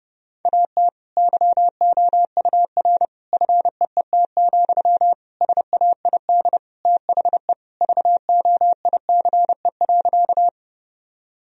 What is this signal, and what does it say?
Morse code, heard as a keyed tone, about 32 wpm, 700 Hz. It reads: AT YOUR FEET, SAID THE VOICE.
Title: Sentences from Top 1000 Words - Review Part 1 - 30wpm